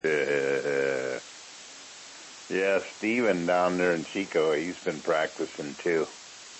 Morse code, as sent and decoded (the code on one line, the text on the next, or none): none